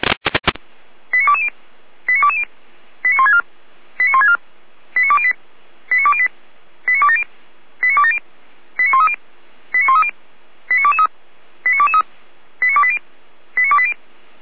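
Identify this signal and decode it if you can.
Five-tone selective call (ZVEI), heard as quick runs of five tones, tones of about 80 ms. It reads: 89220 89220 89164 89164 89208 89208 89270 89270 89130 89130 89223 89223 89280 89280
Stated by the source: ZVEI five-tone selective call 89172 received with an SDR.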